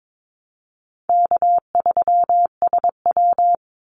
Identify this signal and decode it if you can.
Morse code, heard as a keyed tone, about 22 wpm, 700 Hz. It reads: K3SW
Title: Easy Call Signs Repeated 3x - 22wpm